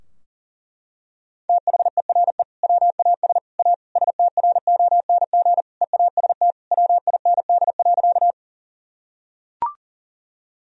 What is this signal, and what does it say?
Morse code, heard as a keyed tone, about 40 wpm, 700 Hz. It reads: THERE WAS A STRONG EAST WIND.
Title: Sentences from Top 1000 Words - Review Part 3 - 40wpm